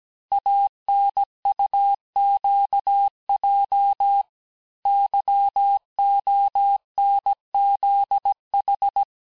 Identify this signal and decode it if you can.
Morse code, heard as a keyed tone, about 17 wpm, 800 Hz. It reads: ANUQJ YONZH
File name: cours-du-lundi files / cours-1587-texte-chiffre-vitesse-16MPM